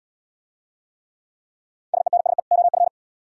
Morse code, heard as a keyed tone, about 74 wpm, 700 Hz. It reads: SENSE OF